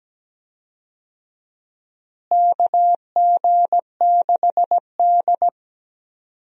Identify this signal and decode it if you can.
Morse code, heard as a keyed tone, about 17 wpm, 700 Hz. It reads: KG6D
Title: USA Call Signs - 4 Characters - Repeated 2x - 17wpm